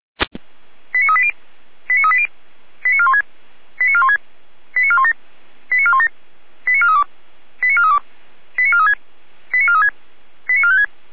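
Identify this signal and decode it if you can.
Five-tone selective call (ZVEI), heard as quick runs of five tones, tones of about 70 ms. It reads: 89380 89380 89416 89416 89417 89417 89432 89432 89436 89436 89456
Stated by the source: ZVEI five-tone call received on SDR for a fire brigade.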